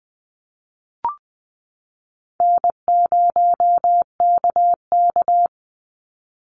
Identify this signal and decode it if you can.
Morse code, heard as a keyed tone, about 20 wpm, 700 Hz. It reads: N0KK
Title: Easy Call Signs - 20wpm